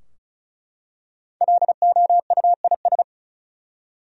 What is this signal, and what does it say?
Morse code, heard as a keyed tone, about 36 wpm, 700 Hz. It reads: LOUIS
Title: USA Common Names - Max 5 Letters - 35wpm